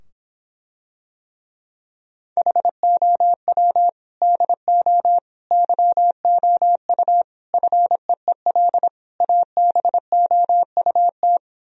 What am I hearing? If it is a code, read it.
Morse code, heard as a keyed tone, about 26 wpm, 700 Hz. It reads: HOW DO YOU FEEL ABOUT